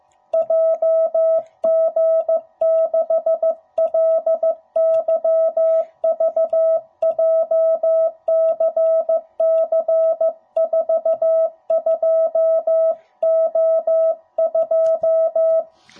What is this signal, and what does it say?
Morse code, heard as a keyed tone, about 15 wpm, 650 Hz. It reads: JG6LYV JCC42O2
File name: jh8cbh / 2021cw / cw05b